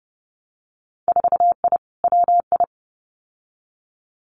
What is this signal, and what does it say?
Morse code, heard as a keyed tone, about 30 wpm, 700 Hz. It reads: 4I WI
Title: Winter Field Day Exchanges 30wpm